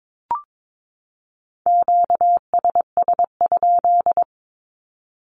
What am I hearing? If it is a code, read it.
Morse code, heard as a keyed tone, about 22 wpm, 700 Hz. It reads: QSS?
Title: Q-Codes 22wpm